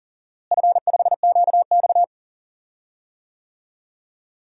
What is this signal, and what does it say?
Morse code, heard as a keyed tone, about 40 wpm, 700 Hz. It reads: F5QX
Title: DXCC-100 Call Signs - 4 Characters - Repeated 2x - 40wpm